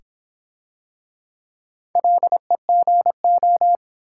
Morse code, heard as a keyed tone, about 23 wpm, 700 Hz. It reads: LEGO
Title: Easy Words - Speed-Racing with Hints - 20wpm